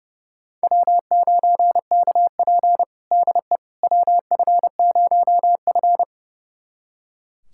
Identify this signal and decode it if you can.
Morse code, heard as a keyed tone, about 30 wpm, 700 Hz. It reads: W9KP DE WF0F
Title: USA Callsign DE Callsign - 4 Characters - 30wpm